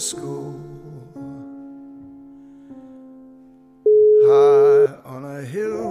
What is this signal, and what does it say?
Ringback tone (Europe, 425 Hz).